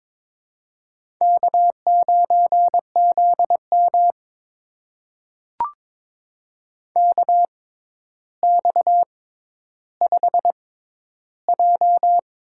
Morse code, heard as a keyed tone, about 22 wpm, spaced as for 11 wpm, 700 Hz. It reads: K9ZM KX5J
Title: USA Call Signs - 4 Characters - 6x Character Spacing - 22wpm